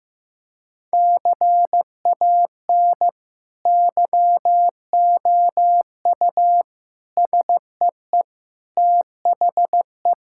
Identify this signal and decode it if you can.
Morse code, heard as a keyed tone, about 16 wpm, 700 Hz. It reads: CAN YOU SEE THE
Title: Sentences from Top 1000 Words - Review Part 5 - 15wpm